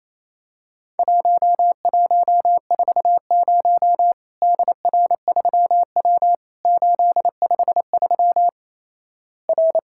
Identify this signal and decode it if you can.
Morse code, heard as a keyed tone, about 28 wpm, 700 Hz. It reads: 1140 DR3W 853 R
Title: Worked All Europe DX Contest - QTC Exchange - 28wpm